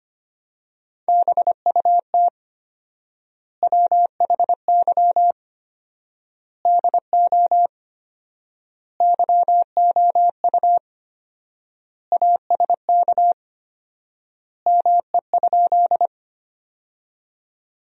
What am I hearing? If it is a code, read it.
Morse code, heard as a keyed tone, about 25 wpm, 700 Hz. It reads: BUT WHY DO YOU ASK ME?
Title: 4x Word-Spacing - Sentences from Top 200 Words - 25wpm